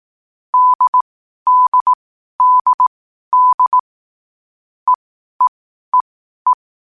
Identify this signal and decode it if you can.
Morse code, heard as a keyed tone, about 18 wpm, spaced as for 12 wpm, 1000 Hz. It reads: DDDD EEEE
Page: Hier findet Ihr das Morsealphabet in langsamer Geschwindigkeit zum Hören. Dabei ist jeder Buchstabe vier Mal hintereinander aufgenommen worden, danach kommt direkt der nächste.